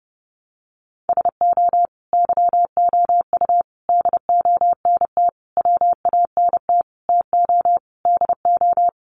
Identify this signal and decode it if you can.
Morse code, heard as a keyed tone, about 30 wpm, 700 Hz. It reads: SO YOU DONT WANT TO DO
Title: Sentences from Top 1000 Words 30wpm